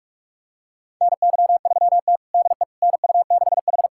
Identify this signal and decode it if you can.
Morse code, heard as a keyed tone, about 46 wpm, 700 Hz. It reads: NY3T DE NU6H
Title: USA Callsign DE Callsign - 4 Characters - 45wpm